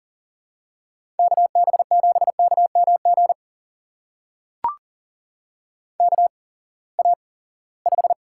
Morse code, heard as a keyed tone, about 40 wpm, spaced as for 23 wpm, 700 Hz. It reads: KB7KMG KA5
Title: USA Call Signs - 6 Characters - 8x Character Spacing - 40wpm